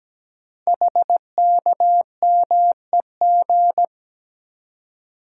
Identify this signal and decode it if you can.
Morse code, heard as a keyed tone, about 17 wpm, 700 Hz. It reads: HKMEG
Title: Code Groups - 5 Letters - 17wpm